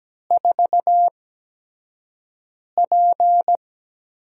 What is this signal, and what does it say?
Morse code, heard as a keyed tone, about 17 wpm, spaced as for 5 wpm, 700 Hz. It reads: 4P
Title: USA Call Signs - 8x Character Spacing - 17wpm